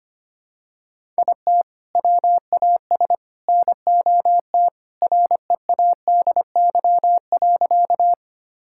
Morse code, heard as a keyed tone, about 25 wpm, 700 Hz. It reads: IT WAS NOT READY.
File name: practice / 25wpm / Sentences with 2-Word Phrases - 25wpm